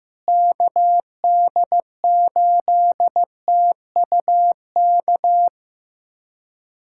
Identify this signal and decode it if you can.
Morse code, heard as a keyed tone, about 15 wpm, 700 Hz. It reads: KD8TUK
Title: USA Callsign DE Callsign 15wpm